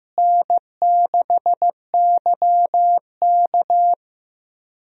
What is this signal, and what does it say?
Morse code, heard as a keyed tone, about 15 wpm, 700 Hz. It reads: N6YK
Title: USA Call Signs - 4 Characters - Repeated 3x - 15wpm